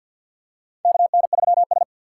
Morse code, heard as a keyed tone, about 47 wpm, 700 Hz. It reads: KN3S
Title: USA Call Signs - 4 Characters - 3x Character Spacing - 50wpm